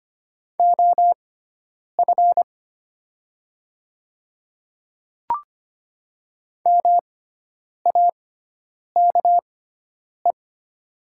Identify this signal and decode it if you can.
Morse code, heard as a keyed tone, about 25 wpm, spaced as for 9 wpm, 700 Hz. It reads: OF MAKE
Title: 6x Character Spacing - Top 100 Words - 25wpm